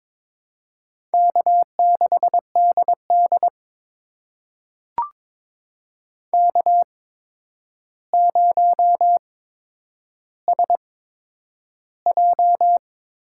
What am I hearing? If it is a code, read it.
Morse code, heard as a keyed tone, about 22 wpm, spaced as for 9 wpm, 700 Hz. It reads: K6DD K0SJ
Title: USA Call Signs - 4 Characters - 8x Character Spacing - 22wpm